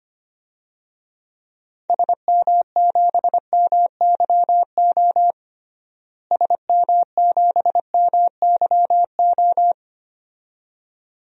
Morse code, heard as a keyed tone, about 25 wpm, 700 Hz. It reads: SM7MYO SM7MYO
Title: DXCC-100 Call Signs - 6 Characters - Repeated 2x - 25wpm